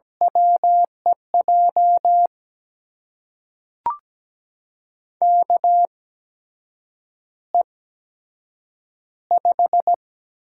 Morse code, heard as a keyed tone, about 17 wpm, spaced as for 7 wpm, 700 Hz. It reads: WEJ KE5
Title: USA Call Signs - 6 Characters - 8x Character Spacing - 17wpm